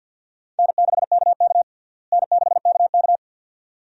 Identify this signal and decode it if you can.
Morse code, heard as a keyed tone, about 52 wpm, 700 Hz. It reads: N6KK N6KK
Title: USA Call Signs - 4 Characters - Repeated 2x - 50wpm